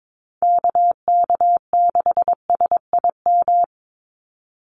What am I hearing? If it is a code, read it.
Morse code, heard as a keyed tone, about 22 wpm, 700 Hz. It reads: KK6SIM
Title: USA Call Signs 22wpm